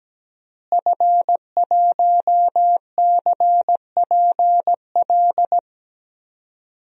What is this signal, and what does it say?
Morse code, heard as a keyed tone, about 17 wpm, 700 Hz. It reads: F1CPL
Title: DXCC-100 Call Signs - 5 Characters - Repeated 2x - 17wpm